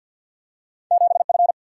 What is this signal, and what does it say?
Morse code, heard as a keyed tone, about 52 wpm, 700 Hz. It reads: 7 F
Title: Single-Letter Single-Number 50wpm